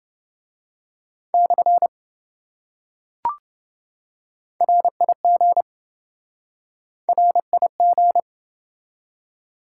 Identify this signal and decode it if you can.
Morse code, heard as a keyed tone, about 26 wpm, 700 Hz. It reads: / RIG RIG